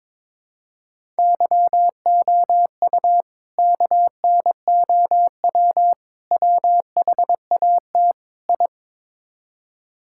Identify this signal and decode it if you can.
Morse code, heard as a keyed tone, about 21 wpm, 700 Hz. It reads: YOU KNOW WHAT I
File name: practice / 22wpm / Build A Phrase - 5-Word Phrases - 22wpm